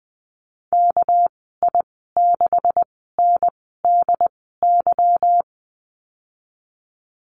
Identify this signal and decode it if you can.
Morse code, heard as a keyed tone, about 20 wpm, spaced as for 14 wpm, 700 Hz. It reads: KI6NDY
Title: 245 Lesson 39g - Letter z - Call Signs 20wpm